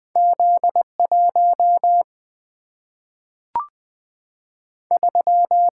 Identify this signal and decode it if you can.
Morse code, heard as a keyed tone, about 20 wpm, 700 Hz. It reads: Z1 3
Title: Single-Letter Single-Number 20wpm